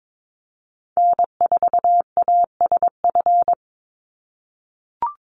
Morse code, heard as a keyed tone, about 22 wpm, 700 Hz. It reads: N4ASF
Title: Call Signs - 22wpm